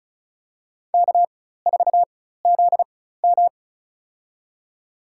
Morse code, heard as a keyed tone, about 34 wpm, spaced as for 16 wpm, 700 Hz. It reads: K4ZM
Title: USA Call Signs - 4 Characters - 4x Character Spacing - 35wpm